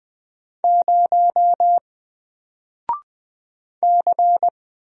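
Morse code, heard as a keyed tone, about 20 wpm, 700 Hz. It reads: 0 C
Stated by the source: Single letter and number Morse drill at 20 wpm, from N4LCD.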